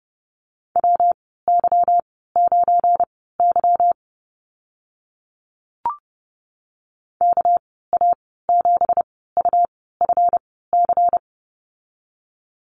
Morse code, heard as a keyed tone, about 30 wpm, spaced as for 17 wpm, 700 Hz. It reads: WY9Y KA7UFC
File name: Lesson 37g - Letter x - Call Signs 30wpm